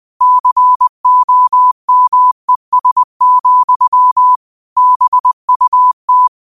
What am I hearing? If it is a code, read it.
Morse code, heard as a keyed tone, about 20 wpm, 1000 Hz. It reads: COMES, BUT
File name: qotdfiles / 20wpm